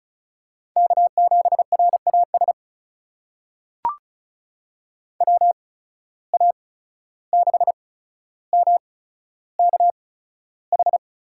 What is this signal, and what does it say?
Morse code, heard as a keyed tone, about 34 wpm, spaced as for 14 wpm, 700 Hz. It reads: K7RAS WA6MKH
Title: USA Call Signs - 8x Character Spacing - 35wpm